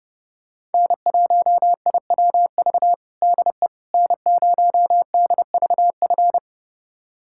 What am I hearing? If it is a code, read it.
Morse code, heard as a keyed tone, about 30 wpm, 700 Hz. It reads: N1IWV DE N0DVF
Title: USA Callsign DE Callsign - 5 Characters - 30wpm